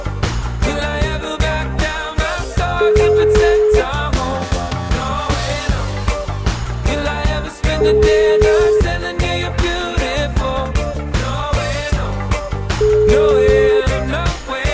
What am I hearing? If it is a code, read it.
Ringback tone (Europe, 425 Hz), with noise.